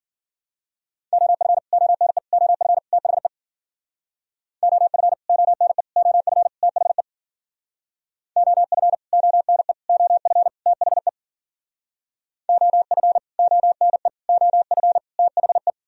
Morse code, heard as a keyed tone, about 54 wpm, 700 Hz. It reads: OF ONE OF THE OF ONE OF THE OF ONE OF THE OF ONE OF THE